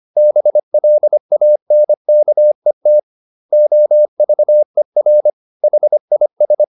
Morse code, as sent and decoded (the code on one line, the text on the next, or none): -... .-.. .- -. -.- . - / --- ...- . .-. / .... .. ...
BLANKET OVER HIS